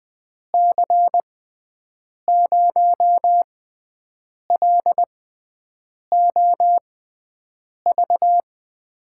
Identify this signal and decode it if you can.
Morse code, heard as a keyed tone, about 20 wpm, spaced as for 7 wpm, 700 Hz. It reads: C0LOV
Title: USA Call Signs - 6 Characters - 6x Character Spacing - 20wpm